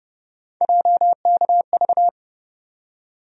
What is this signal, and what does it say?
Morse code, heard as a keyed tone, about 32 wpm, 700 Hz. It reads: JKV
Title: USA Call Sign Postfixes - 3 Letters 30wpm